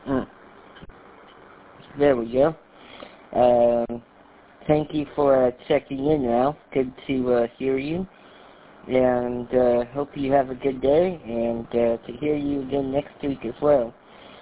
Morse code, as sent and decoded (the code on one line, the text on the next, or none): none